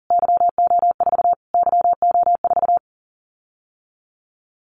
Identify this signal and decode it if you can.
Morse code, heard as a keyed tone, about 40 wpm, 700 Hz. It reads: YO4 YO4
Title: Callsign Prefixes for Top 25 DXCC - Repeated 3x - 40wpm